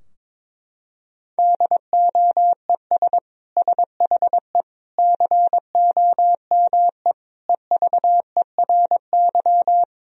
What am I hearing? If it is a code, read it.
Morse code, heard as a keyed tone, about 21 wpm, 700 Hz. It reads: DOES SHE COME EVERY
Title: Sentences from Top 1000 Words - Review Part 4 - 22wpm